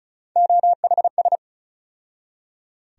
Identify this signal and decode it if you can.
Morse code, heard as a keyed tone, about 34 wpm, 700 Hz. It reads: OHS
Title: USA Call Sign Postfixes - 3 Letters 35wpm